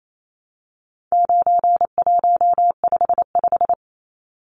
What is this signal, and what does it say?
Morse code, heard as a keyed tone, about 28 wpm, 700 Hz. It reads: 9155
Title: Numbers - 4-Digit 28wpm